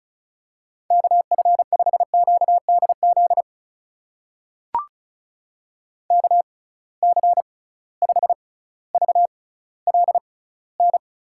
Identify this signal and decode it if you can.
Morse code, heard as a keyed tone, about 36 wpm, spaced as for 18 wpm, 690 Hz. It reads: KF5QDZ KC5VLN
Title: USA Call Signs - 6 Characters - 6x Character Spacing - 35wpm